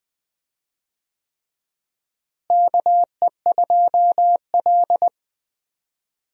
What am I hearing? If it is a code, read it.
Morse code, heard as a keyed tone, about 21 wpm, 700 Hz. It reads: KE2L
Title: USA Call Signs - 4 Characters - Repeated 3x - 20wpm